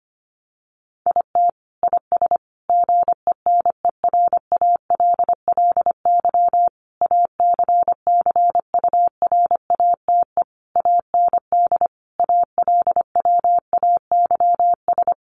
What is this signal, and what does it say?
Morse code, heard as a keyed tone, about 25 wpm, 700 Hz. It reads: IT IS GENERALLY ACCURATE AND ALWAYS
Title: Sentences from Top 1200 Words 25wpm